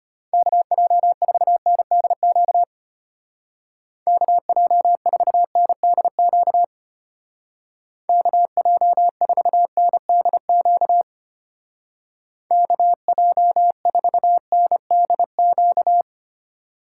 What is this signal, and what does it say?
Morse code, heard as a keyed tone, about 32 wpm, 700 Hz. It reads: KJ4NDQ KJ4NDQ KJ4NDQ KJ4NDQ